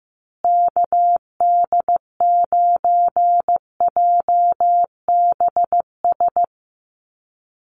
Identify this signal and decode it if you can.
Morse code, heard as a keyed tone, about 15 wpm, 700 Hz. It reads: KD9JBS
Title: USA Call Signs Repeated 3x 15wpm